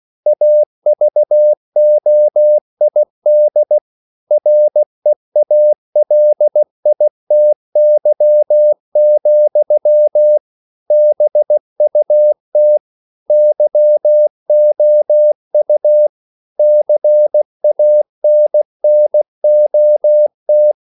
Morse code, heard as a keyed tone, about 16 wpm, 600 Hz. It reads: AVOID REALITY, BUT YOU CANNOT
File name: CW / Practice / EnglishQuotations0116